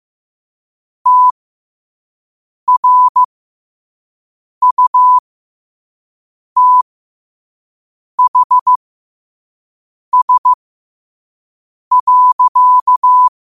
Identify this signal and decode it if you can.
Morse code, heard as a keyed tone, about 15 wpm, spaced as for 5 wpm, 1000 Hz. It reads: TRUTHS.
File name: qotdfiles / 5wpm